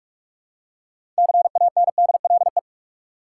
Morse code, heard as a keyed tone, about 46 wpm, 700 Hz. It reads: CANDLE
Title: Six Letter Words - 45wpm